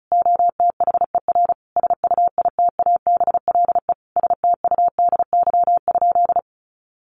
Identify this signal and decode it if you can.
Morse code, heard as a keyed tone, about 34 wpm, 700 Hz. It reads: OTHER SUITABLE STUDY?
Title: Sentences from Top 1400 Words 35wpm